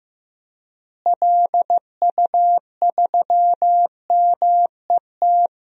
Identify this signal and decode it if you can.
Morse code, heard as a keyed tone, about 15 wpm, 700 Hz. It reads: LU3MET
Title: DXCC-100 Call Signs - 6 Characters - Repeated 3x - 15wpm